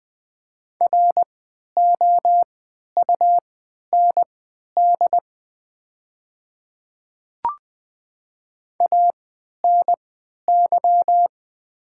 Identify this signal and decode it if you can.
Morse code, heard as a keyed tone, about 20 wpm, spaced as for 11 wpm, 700 Hz. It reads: ROUND ANY
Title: 3x Character Spacing - Top 300 Words - 20wpm